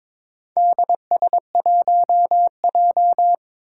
Morse code, heard as a keyed tone, about 22 wpm, 700 Hz. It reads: DS1J
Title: Four Challenging Letter-Number Combinations 22wpm